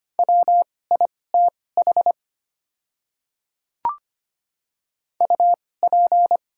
Morse code, heard as a keyed tone, about 25 wpm, spaced as for 18 wpm, 700 Hz. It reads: WITH UP
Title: 2x Character Spacing - Top 100 Words - 25wpm